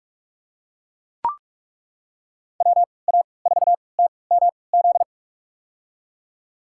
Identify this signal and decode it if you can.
Morse code, heard as a keyed tone, about 42 wpm, spaced as for 25 wpm, 700 Hz. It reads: WA4TMZ
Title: USA Call Signs - 6 Characters - 3x Character Spacing - 45wpm